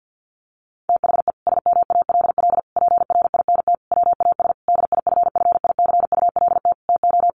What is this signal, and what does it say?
Morse code, heard as a keyed tone, about 50 wpm, 700 Hz. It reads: THE SMALL PRINT WAS DIFFICULT TO
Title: Sentences from Top 1400 Words 50wpm